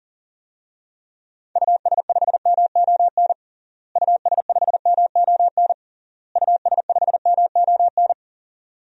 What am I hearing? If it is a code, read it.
Morse code, heard as a keyed tone, about 40 wpm, 700 Hz. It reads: US5MON US5MON US5MON
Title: DXCC-100 Call Signs - 6 Characters - Repeated 3x - 40wpm